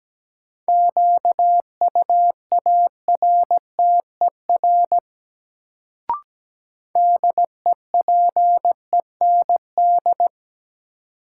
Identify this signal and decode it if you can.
Morse code, heard as a keyed tone, about 17 wpm, 700 Hz. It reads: QUARTER DEPEND
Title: Sentences from Top 1300 Words - New Word Drill - 17wpm